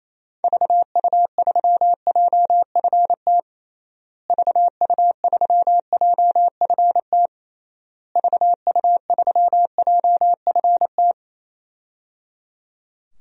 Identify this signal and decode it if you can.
Morse code, heard as a keyed tone, about 28 wpm, 700 Hz. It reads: VU3JFT VU3JFT VU3JFT
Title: DXCC-100 Call Signs - 6 Characters - Repeated 3x - 28wpm